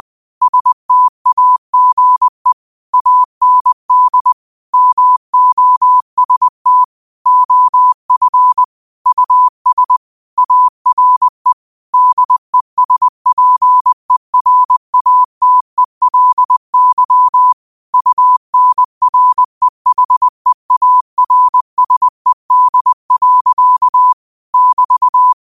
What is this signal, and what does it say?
Morse code, heard as a keyed tone, about 20 wpm, 1000 Hz. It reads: STAGE AND MOST OF US ARE DESPERATELY UNREHEARSED. <BT>